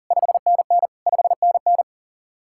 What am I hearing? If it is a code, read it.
Morse code, heard as a keyed tone, about 52 wpm, 700 Hz. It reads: 5NN 5NN